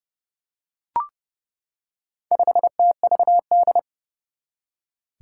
Morse code, heard as a keyed tone, about 30 wpm, 700 Hz. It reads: 5TVD
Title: Four Challenging Letter-Number Combinations 30wpm